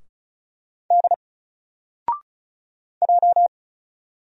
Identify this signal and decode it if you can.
Morse code, heard as a keyed tone, about 34 wpm, 700 Hz. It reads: D J
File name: Single-Letter-Number-35wpm